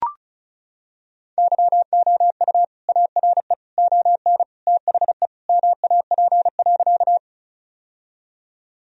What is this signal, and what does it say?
Morse code, heard as a keyed tone, about 36 wpm, 700 Hz. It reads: YOU ARE ON THE MAP.
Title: Sentences from Top 1000 Words - Review Part 2 - 35wpm